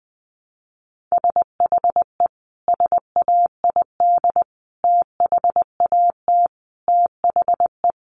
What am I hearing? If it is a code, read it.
Morse code, heard as a keyed tone, about 20 wpm, 700 Hz. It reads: SHE SAID THAT THE
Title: Sentences from Top 1200 Words 20wpm